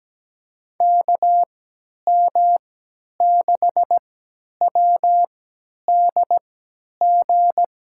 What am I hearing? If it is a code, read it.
Morse code, heard as a keyed tone, about 17 wpm, spaced as for 10 wpm, 700 Hz. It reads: KM6WDG
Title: USA Call Signs - 6 Characters - 3x Character Spacing - 17wpm